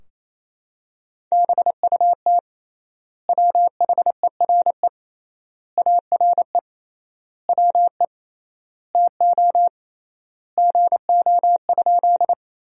Morse code, heard as a keyed tone, about 28 wpm, 700 Hz. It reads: BUT WHERE ARE WE TO GO?
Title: 3x Word-Spacing - Sentences from Top 200 Words - 28wpm